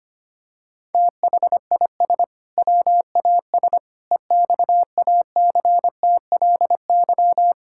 Morse code, heard as a keyed tone, about 25 wpm, 700 Hz. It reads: THIS WAS EXACTLY
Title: Sentences from Top 1000 Words - Review Part 6 - 25wpm